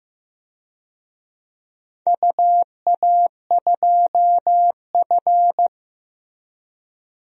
Morse code, heard as a keyed tone, about 15 wpm, 700 Hz. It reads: UA2F
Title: DXCC-100 Call Signs - 4 Characters - Repeated 3x - 15wpm